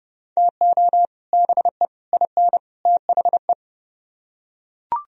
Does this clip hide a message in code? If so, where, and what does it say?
Yes: Morse code from 0.4 to 3.5 s, heard as a keyed tone, about 30 wpm, 700 Hz. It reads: TO BE IN THE